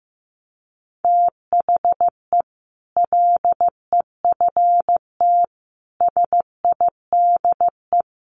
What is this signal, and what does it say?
Morse code, heard as a keyed tone, about 14 wpm, 700 Hz. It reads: THE LEFT SIDE
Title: Sets of 5 Words Encore 15wpm